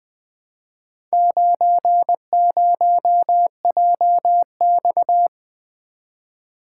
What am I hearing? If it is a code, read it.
Morse code, heard as a keyed tone, about 20 wpm, 700 Hz. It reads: 90JX
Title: Four Letter-Number Combinations - 20wpm